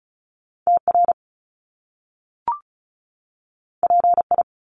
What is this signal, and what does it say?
Morse code, heard as a keyed tone, about 36 wpm, 700 Hz. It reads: TR PI
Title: Top Two Letter Combinations in Words 35wpm